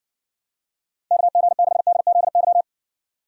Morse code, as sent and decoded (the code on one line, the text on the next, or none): -.- --. -.... -.. --.. -.--
KG6DZY